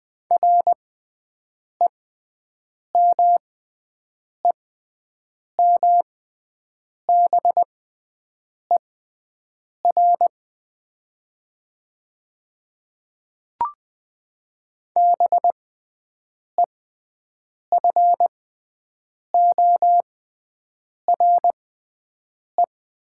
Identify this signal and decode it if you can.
Morse code, heard as a keyed tone, about 20 wpm, spaced as for 7 wpm, 700 Hz. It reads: REMEMBER BEFORE
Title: 6x Character Spacing - Top 500 Words - 20wpm